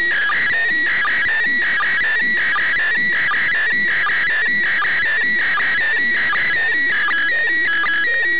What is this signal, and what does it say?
SSTV (slow-scan television), PD 180.
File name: suws-websdr recording t18-13-19z -8khz